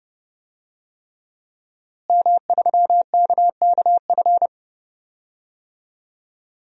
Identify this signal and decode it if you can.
Morse code, heard as a keyed tone, about 30 wpm, 700 Hz. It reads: M3KKF
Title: DXCC-100 Call Signs - 5 Characters - Repeated 3x - 30wpm